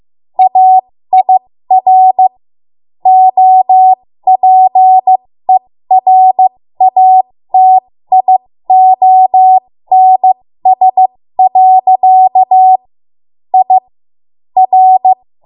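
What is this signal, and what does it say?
Morse code, heard as a keyed tone, about 16 wpm, spaced as for 13 wpm, 750 Hz. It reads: AIR OPERATIONS. I R
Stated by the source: 13 wpm Morse code practice archive transmission.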